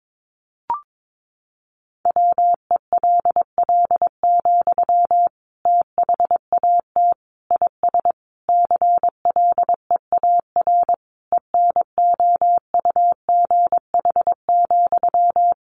Morse code, heard as a keyed tone, about 22 wpm, 700 Hz. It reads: WELL, THAT IS CLEAR ENOUGH,
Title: Sentences from Top 900 Words 22wpm